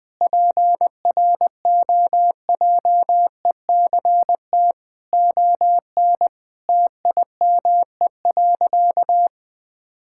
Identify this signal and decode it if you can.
Morse code, heard as a keyed tone, about 20 wpm, 700 Hz. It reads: PROJECT ON TIME.
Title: Difficult Sentences with Hints - 20wpm